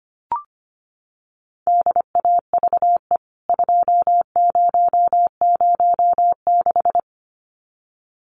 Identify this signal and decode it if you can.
Morse code, heard as a keyed tone, about 25 wpm, 700 Hz. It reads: DAVE 2006
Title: CWT Exchange 25wpm